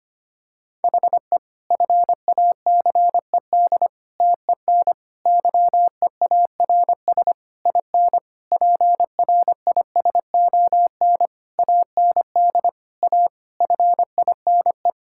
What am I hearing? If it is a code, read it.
Morse code, heard as a keyed tone, about 25 wpm, 700 Hz. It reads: HE FACED TEN YEARS IN PRISON AND A FINE
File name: practice / 25wpm / Sentences with 9-Word Phrases - 25wpm